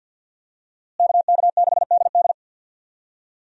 Morse code, heard as a keyed tone, about 47 wpm, 700 Hz. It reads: KK6DD
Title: USA Call Signs - 5 Characters - Repeated 2x - 50wpm